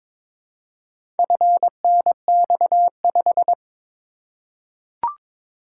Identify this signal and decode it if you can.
Morse code, heard as a keyed tone, about 22 wpm, 700 Hz. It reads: FNX5
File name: practice / 22wpm / Four Letter-Number Combinations - 22wpm